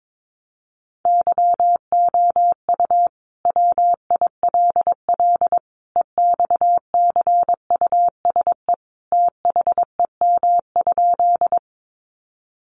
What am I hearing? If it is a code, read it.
Morse code, heard as a keyed tone, about 22 wpm, 700 Hz. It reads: YOU WILL EXCUSE THEM?
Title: Sentences from Top 700 Words 22wpm